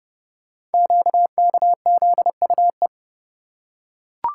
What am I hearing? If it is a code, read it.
Morse code, heard as a keyed tone, about 28 wpm, 690 Hz. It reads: QKZUE
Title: Code Groups - 5 Letters - 30wpm